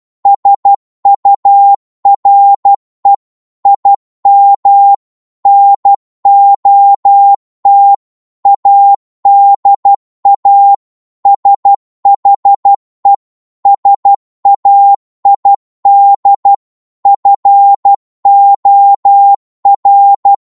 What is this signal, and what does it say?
Morse code, heard as a keyed tone, about 12 wpm, 800 Hz. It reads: SURE IM NOT ADA SHE SAID FOR
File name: 12wpm